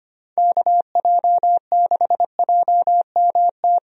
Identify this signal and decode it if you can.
Morse code, heard as a keyed tone, about 25 wpm, 700 Hz. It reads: KJ6JMT
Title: USA Call Signs - 6 Characters - 25wpm